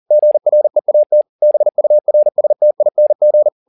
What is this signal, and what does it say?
Morse code, heard as a keyed tone, about 40 wpm, 600 Hz. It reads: GREAT BURSTING
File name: the old man and the sea 2 40wpm